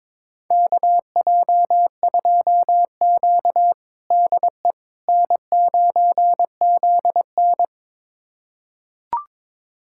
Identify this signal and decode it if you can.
Morse code, heard as a keyed tone, about 22 wpm, 700 Hz. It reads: KJ2Q DE N9ZN
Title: USA Callsign DE Callsign - 4 Characters - 22wpm